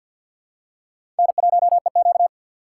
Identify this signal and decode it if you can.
Morse code, heard as a keyed tone, about 52 wpm, 700 Hz. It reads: N1EQ
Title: USA Call Signs - 4 Characters - 3x Character Spacing - 50wpm